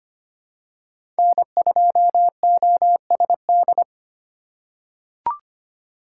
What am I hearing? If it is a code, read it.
Morse code, heard as a keyed tone, about 25 wpm, 700 Hz. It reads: N2OSD
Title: USA Call Signs - 5 Characters - 4x Character Spacing - 25wpm